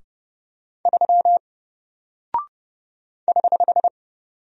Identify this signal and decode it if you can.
Morse code, heard as a keyed tone, about 30 wpm, 700 Hz. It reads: 3 <HH>
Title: Single Character (Letter:Number:Punctuation:Prosign) Alt 30wpm